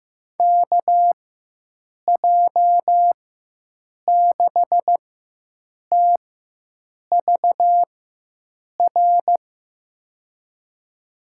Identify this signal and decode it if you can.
Morse code, heard as a keyed tone, about 15 wpm, spaced as for 7 wpm, 700 Hz. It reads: KJ6TVR
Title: USA Call Signs - 6 Characters - 4x Character Spacing - 15wpm